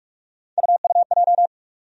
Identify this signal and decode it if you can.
Morse code, heard as a keyed tone, about 48 wpm, 700 Hz. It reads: UUJ
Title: USA Call Sign Postfixes - 3 Letters 45wpm